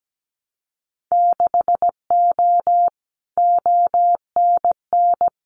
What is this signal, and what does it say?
Morse code, heard as a keyed tone, about 17 wpm, 700 Hz. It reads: 6O ONN
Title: Winter Field Day Exchanges Encore 17wpm